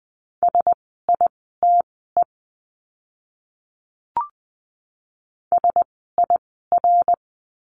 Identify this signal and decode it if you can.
Morse code, heard as a keyed tone, about 20 wpm, spaced as for 15 wpm, 700 Hz. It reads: SITE SIR
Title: 061 Lesson 11f - Letter h - Words 20wpm